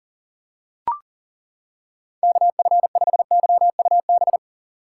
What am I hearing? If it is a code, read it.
Morse code, heard as a keyed tone, about 40 wpm, 700 Hz. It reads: KF5YUB
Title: K1USN SST QSO Practice - 40wpm